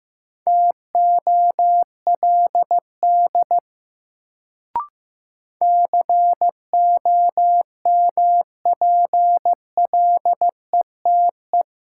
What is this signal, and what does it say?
Morse code, heard as a keyed tone, about 15 wpm, 700 Hz. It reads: TOLD COMPLETE
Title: Sentences from Top 400 Words - New Word Drill - 15wpm